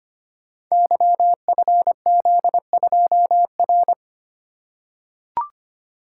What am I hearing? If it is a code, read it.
Morse code, heard as a keyed tone, about 25 wpm, 700 Hz. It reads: YFZ2R
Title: Code Groups - 5 Letter:Numbers - 25wpm